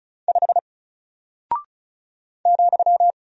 Morse code, heard as a keyed tone, about 34 wpm, 700 Hz. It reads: 5 ,
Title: Single Character (Letter:Number:Punctuation:Prosign) Alt 35wpm